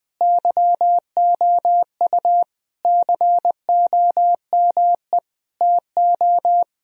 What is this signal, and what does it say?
Morse code, heard as a keyed tone, about 20 wpm, 700 Hz. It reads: YOU COME TO
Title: Sentences from Top 1000 Words - Review Part 2 - 20wpm